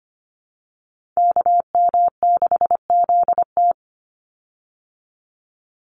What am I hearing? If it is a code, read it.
Morse code, heard as a keyed tone, about 25 wpm, 700 Hz. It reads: KM6ZT
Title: Call Signs Repeated 2x - 25wpm